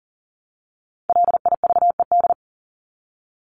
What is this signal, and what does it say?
Morse code, heard as a keyed tone, about 40 wpm, 700 Hz. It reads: LIVED
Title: ED Suffix - Top 100 Words with Base Words 40wpm